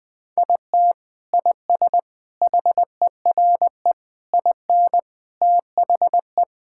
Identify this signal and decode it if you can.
Morse code, heard as a keyed tone, about 20 wpm, 700 Hz. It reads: IT IS HERE IN THE
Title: Sentences from Top 1600 Words 20wpm